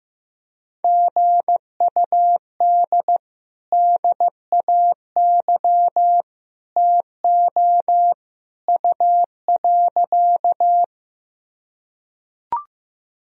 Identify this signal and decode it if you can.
Morse code, heard as a keyed tone, about 14 wpm, 700 Hz. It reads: GUD DAY TO U.
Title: Sentences from Top 200 Words and Abbreviations 15wpm